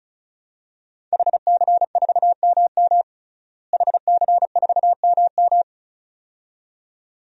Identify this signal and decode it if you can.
Morse code, heard as a keyed tone, about 34 wpm, 700 Hz. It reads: HC4MM HC4MM
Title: DXCC-100 Call Signs - 5 Characters - Repeated 2x - 35wpm